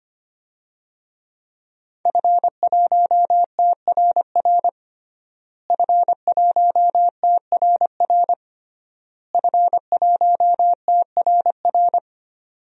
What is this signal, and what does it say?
Morse code, heard as a keyed tone, about 25 wpm, 700 Hz. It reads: F1TRR F1TRR F1TRR
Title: DXCC-100 Call Signs - 5 Characters - Repeated 3x - 25wpm